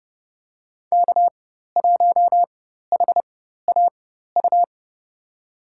Morse code, heard as a keyed tone, about 32 wpm, spaced as for 14 wpm, 700 Hz. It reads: K1HAU
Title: USA Call Signs - 4x Character Spacing - 30wpm